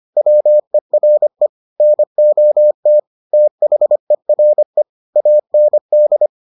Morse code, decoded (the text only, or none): WERE NOT THERE AND